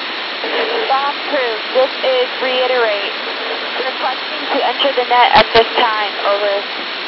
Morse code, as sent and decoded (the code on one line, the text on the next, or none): none